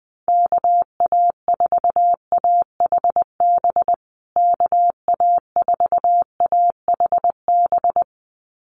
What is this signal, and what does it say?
Morse code, heard as a keyed tone, about 20 wpm, 700 Hz. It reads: KA4AHB KA4AHB
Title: USA Call Signs Repeated 2x 20wpm